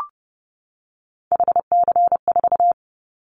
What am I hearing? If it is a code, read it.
Morse code, heard as a keyed tone, about 30 wpm, 700 Hz. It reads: HC4
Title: Callsign Prefixes for Top 100 DXCC - 30wpm